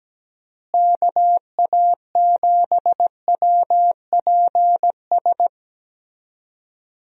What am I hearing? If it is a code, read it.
Morse code, heard as a keyed tone, about 17 wpm, 700 Hz. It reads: KA7WPS
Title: USA Call Signs - 6 Characters - Repeated 2x - 17wpm